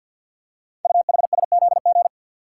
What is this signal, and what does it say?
Morse code, heard as a keyed tone, about 51 wpm, 700 Hz. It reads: UHSZG
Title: Code Groups - 5 Letters - 50wpm